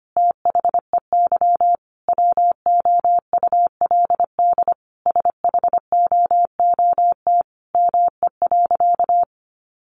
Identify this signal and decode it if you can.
Morse code, heard as a keyed tone, about 25 wpm, 700 Hz. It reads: THEY WOULD SHOOT ME.
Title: Sentences from Top 1500 Words 25wpm